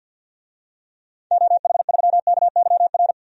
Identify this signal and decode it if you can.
Morse code, heard as a keyed tone, about 53 wpm, 700 Hz. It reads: OH3KYR
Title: DXCC-100 Call Signs - 6 Characters - 50wpm